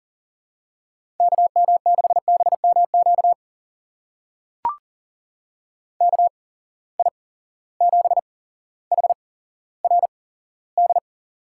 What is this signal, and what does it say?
Morse code, heard as a keyed tone, about 40 wpm, spaced as for 17 wpm, 700 Hz. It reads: KM6BMQ KI7HRD
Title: USA Call Signs - 6 Characters - 8x Character Spacing - 40wpm